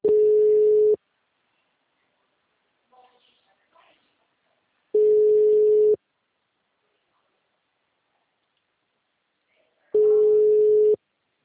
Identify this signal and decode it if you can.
Ringback tone (Europe, 425 Hz), with speech.
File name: out- -2006- - - .14057